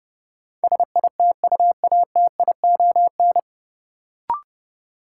Morse code, heard as a keyed tone, about 30 wpm, 700 Hz. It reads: SITUATION
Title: Sentences from Top 1200 Words - New Word Drill - 30wpm